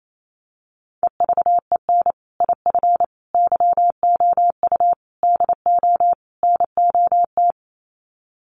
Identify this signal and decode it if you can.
Morse code, heard as a keyed tone, about 28 wpm, 700 Hz. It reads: EVEN IF YOU DO NOT
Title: Sets of 5 Words Encore 28wpm